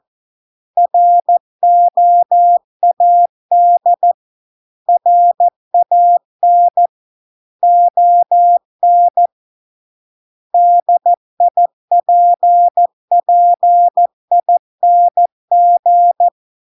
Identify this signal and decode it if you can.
Morse code, heard as a keyed tone, about 14 wpm, 700 Hz. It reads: ROAD RAN ON DIPPING